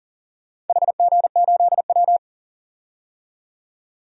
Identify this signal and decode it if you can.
Morse code, heard as a keyed tone, about 41 wpm, 700 Hz. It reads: HG8W
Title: DXCC-100 Call Signs - 4 Characters - 40wpm